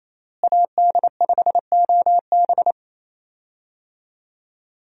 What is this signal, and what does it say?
Morse code, heard as a keyed tone, about 28 wpm, 700 Hz. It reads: AD5OB
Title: USA Call Signs - 5 Characters - Repeated 2x - 28wpm